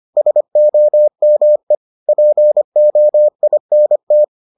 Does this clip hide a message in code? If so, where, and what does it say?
Yes: Morse code from 0.2 to 4.2 s, heard as a keyed tone, about 25 wpm, 600 Hz. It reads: SOME POINT